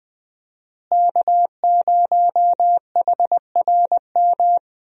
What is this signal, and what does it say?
Morse code, heard as a keyed tone, about 21 wpm, 700 Hz. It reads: K0HRM
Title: USA Call Signs - 5 Characters - Repeated 2x - 20wpm